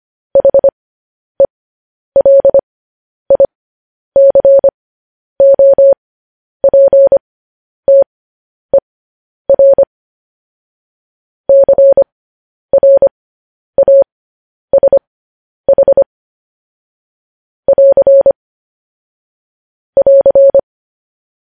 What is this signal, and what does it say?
Morse code, heard as a keyed tone, about 25 wpm, spaced as for 10 wpm, 550 Hz. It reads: HELICOPTER CRASH <AR> <AR>
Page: news headlines in morse code at 10 wpm